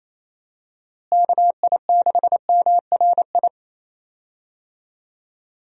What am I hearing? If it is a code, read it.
Morse code, heard as a keyed tone, about 28 wpm, 700 Hz. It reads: KI6MRI
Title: USA Call Signs - 6 Characters - Repeated 3x - 28wpm